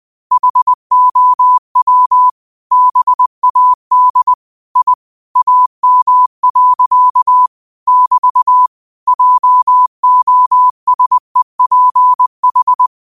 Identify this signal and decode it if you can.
Morse code, heard as a keyed tone, about 20 wpm, 1000 Hz. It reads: HOW BAD I AM. <BT> JOSEPH
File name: 20wpm